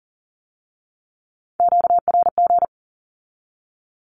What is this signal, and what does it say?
Morse code, heard as a keyed tone, about 43 wpm, 700 Hz. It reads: QRG
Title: Q-Codes with Spoken Meaning - 40wpm